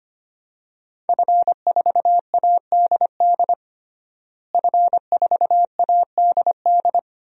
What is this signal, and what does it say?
Morse code, heard as a keyed tone, about 25 wpm, 700 Hz. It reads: F4ADD F4ADD
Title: DXCC-100 Call Signs - 5 Characters - Repeated 2x - 25wpm